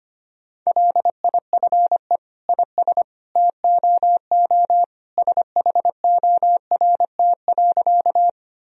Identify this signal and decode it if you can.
Morse code, heard as a keyed tone, about 25 wpm, 700 Hz. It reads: LIFE IS TOO SHORT.
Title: Sentences from Top 1000 Words - Review Part 2 - 25wpm